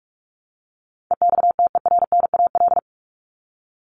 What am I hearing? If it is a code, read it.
Morse code, heard as a keyed tone, about 44 wpm, 700 Hz. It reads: EXTERNAL
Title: Top 1751-2000 Words 45wpm